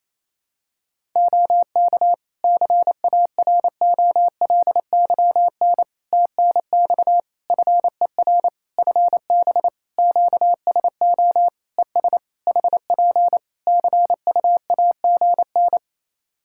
Morse code, heard as a keyed tone, about 27 wpm, 700 Hz. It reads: OK CAROLYN TNX FER FB QSO ES HP CUAGN
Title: Practice QSO - Exchange 4 - Ending - 28wpm